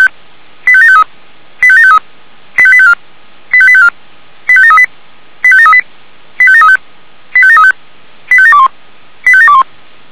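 Five-tone selective call (ZVEI), heard as quick runs of five tones, tones of about 70 ms. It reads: 85753 85753 85854 85854 85738 85738 85735 85735 85721 85721
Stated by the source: ZVEI five-tone call received by SDR.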